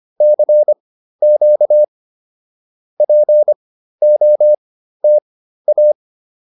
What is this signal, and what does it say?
Morse code, heard as a keyed tone, about 25 wpm, spaced as for 13 wpm, 600 Hz. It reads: CQ POTA
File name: POTA207 13